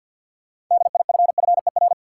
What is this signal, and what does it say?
Morse code, heard as a keyed tone, about 50 wpm, 700 Hz. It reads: DIFFER
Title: Sentences from Top 1000 Words - Review Part 1 - New Word Drill - 50wpm